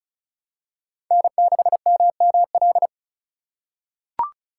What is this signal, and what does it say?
Morse code, heard as a keyed tone, about 34 wpm, 700 Hz. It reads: N6MML
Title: USA Call Signs - 5 Characters - 2x Character Spacing - 35wpm